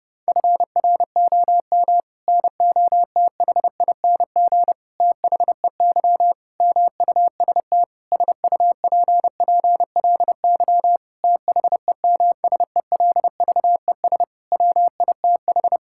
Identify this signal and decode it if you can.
Morse code, heard as a keyed tone, about 30 wpm, 700 Hz. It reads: FROM NOTHING THEY MUST SUPPLY THEMSELVES WITH
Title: Sentences from Top 1000 Words - Review Part 5 - 30wpm